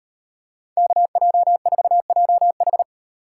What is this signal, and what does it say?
Morse code, heard as a keyed tone, about 35 wpm, 700 Hz. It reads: KJ4JH